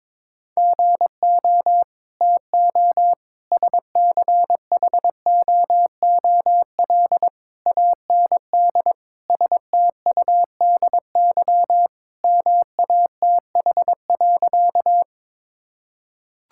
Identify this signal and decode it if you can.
Morse code, heard as a keyed tone, about 22 wpm, 700 Hz. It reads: GO TO SCHOOL AND STUDY MATH.
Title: Sentences from Top 1000 Words - Review Part 6 - 22wpm